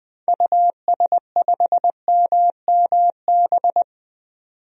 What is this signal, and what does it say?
Morse code, heard as a keyed tone, about 21 wpm, 700 Hz. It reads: US5MMB
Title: DXCC-100 Call Signs - 6 Characters - 20wpm